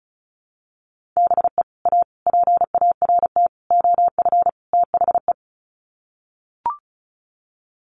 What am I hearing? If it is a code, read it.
Morse code, heard as a keyed tone, about 34 wpm, 700 Hz. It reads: BE A PART OF THE
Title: Sets of 5 Words Encore 35wpm